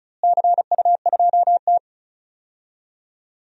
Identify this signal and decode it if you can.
Morse code, heard as a keyed tone, about 34 wpm, 700 Hz. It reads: CU2T
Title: DXCC-100 Call Signs - 4 Characters - Repeated 3x - 35wpm